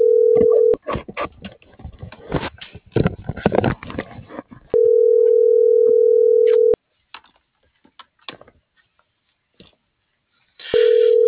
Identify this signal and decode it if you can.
Ringback tone (North America), 440 + 480 Hz.